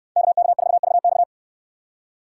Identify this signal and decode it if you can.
Morse code, heard as a keyed tone, about 63 wpm, 700 Hz. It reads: KK4VX